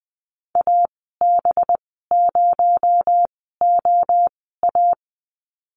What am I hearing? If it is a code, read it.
Morse code, heard as a keyed tone, about 20 wpm, spaced as for 14 wpm, 700 Hz. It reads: AB0OA
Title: 231 Lesson 37g - Letter x - Call Signs 20wpm